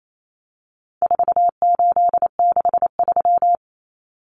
Morse code, heard as a keyed tone, about 28 wpm, 700 Hz. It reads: 4863
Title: Common Cut Numbers - 4-Digit 28wpm